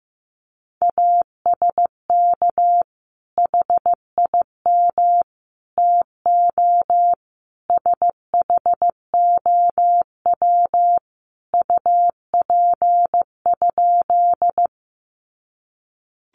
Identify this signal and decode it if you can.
Morse code, heard as a keyed tone, about 15 wpm, 700 Hz. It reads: ASK HIM TO SHOW UP?
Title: Sentences from Top 200 Words 15wpm